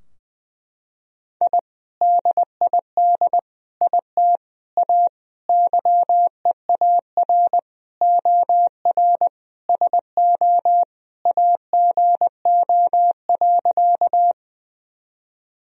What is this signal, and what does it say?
Morse code, heard as a keyed tone, about 20 wpm, 700 Hz. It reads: I DID IT A YEAR OR SO AGO.
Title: Sentences from Top 1000 Words - Review Part 3 - 20wpm